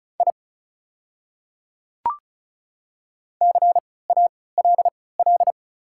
Morse code, heard as a keyed tone, about 34 wpm, spaced as for 20 wpm, 700 Hz. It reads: I CALL
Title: 3x Character Spacing - Top 200 Words - 35wpm